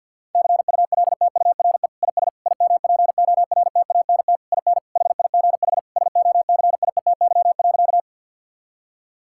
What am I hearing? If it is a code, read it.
Morse code, heard as a keyed tone, about 50 wpm, 700 Hz. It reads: CULTURE IS IMPORTANT IN HIGH SOCIETY.
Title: Sentences from Top 1000 Words - Review Part 6 - 50wpm